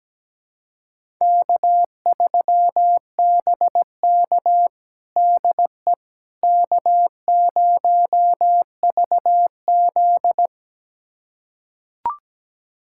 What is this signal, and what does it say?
Morse code, heard as a keyed tone, about 17 wpm, 700 Hz. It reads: K3BK DE K0VZ
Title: USA Callsign DE Callsign - 4 Characters - 17wpm